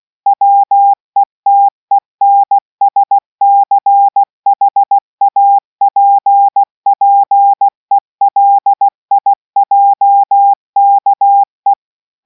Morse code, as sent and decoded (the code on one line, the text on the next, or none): .-- . - . -. ... -.-. .... .- .--. .--. .-.. .. .--- -.- .
WETENSCHAPPELIJKE